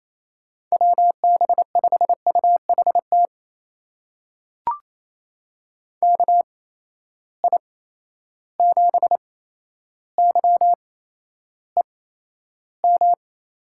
Morse code, heard as a keyed tone, about 28 wpm, spaced as for 12 wpm, 700 Hz. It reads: WB5UHT KI7YEM
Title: USA Call Signs - 6 Characters - 8x Character Spacing - 28wpm